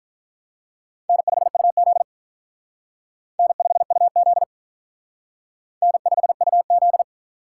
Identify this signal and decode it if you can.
Morse code, heard as a keyed tone, about 45 wpm, 700 Hz. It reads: N5UZ N5UZ N5UZ